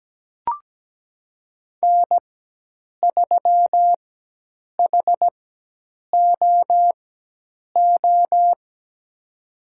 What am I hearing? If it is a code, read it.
Morse code, heard as a keyed tone, about 17 wpm, spaced as for 8 wpm, 700 Hz. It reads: N3HOO
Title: USA Call Signs - 5 Characters - 4x Character Spacing - 17wpm